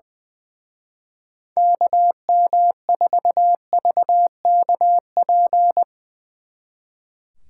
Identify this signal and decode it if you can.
Morse code, heard as a keyed tone, about 20 wpm, 700 Hz. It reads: KM4VKP
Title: Speed Racing - USA 2x3 Call Signs 20wpm